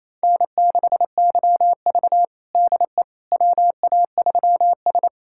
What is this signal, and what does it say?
Morse code, heard as a keyed tone, about 28 wpm, 700 Hz. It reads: N6YV DE WA3S
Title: USA Callsign DE Callsign - 4 Characters - 28wpm